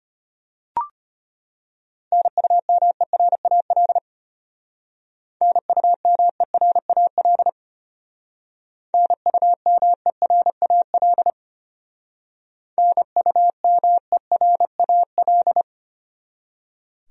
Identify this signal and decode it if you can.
Morse code, heard as a keyed tone, about 31 wpm, 700 Hz. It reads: NUMERAL NUMERAL NUMERAL NUMERAL